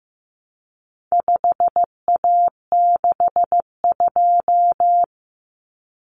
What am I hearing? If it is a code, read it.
Morse code, heard as a keyed tone, about 15 wpm, 700 Hz. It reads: 5A62
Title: Common Cut Numbers - 4-Digit 15wpm